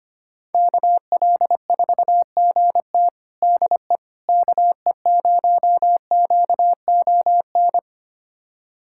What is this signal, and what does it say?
Morse code, heard as a keyed tone, about 25 wpm, 700 Hz. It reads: KL4GT DE KE0QON
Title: USA Callsign DE Callsign 25wpm